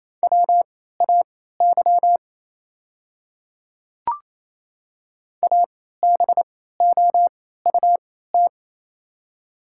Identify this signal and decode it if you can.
Morse code, heard as a keyed tone, about 27 wpm, spaced as for 16 wpm, 700 Hz. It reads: WAY ABOUT
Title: 3x Character Spacing - Top 200 Words - 28wpm